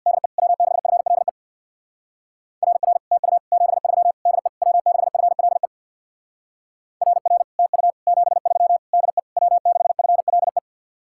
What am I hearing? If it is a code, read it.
Morse code, heard as a keyed tone, about 62 wpm, 700 Hz. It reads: DE W6FLE RR TU 73 DE W6FLE RR TU 73 DE W6FLE
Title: Speed-Racing Common QSO Phrases Encore - 50wpm